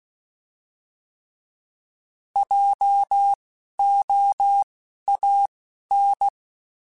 Morse code, heard as a keyed tone, about 16 wpm, spaced as for 11 wpm, 800 Hz. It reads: JOAN
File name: Alphabetical / Alphabetical mp3s / Js @10